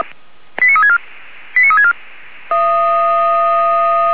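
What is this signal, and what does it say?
Five-tone selective call (ZVEI), heard as quick runs of five tones, tones of about 70 ms. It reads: 89364 89364
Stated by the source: ZVEI five-tone call received by SDR.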